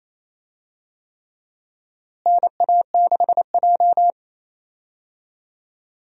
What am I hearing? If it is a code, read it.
Morse code, heard as a keyed tone, about 28 wpm, 700 Hz. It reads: NA6J
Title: USA Call Signs - 4 Characters - Repeated 3x - 28wpm